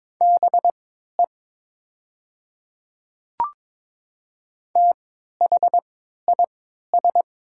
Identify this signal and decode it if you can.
Morse code, heard as a keyed tone, about 22 wpm, spaced as for 13 wpm, 700 Hz. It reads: BE THIS
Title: 3x Character Spacing - Top 100 Words - 22wpm